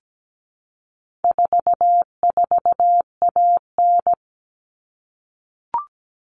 Morse code, heard as a keyed tone, about 17 wpm, 700 Hz. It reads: 44AN
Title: All Cut Numbers - 4-Digit 17wpm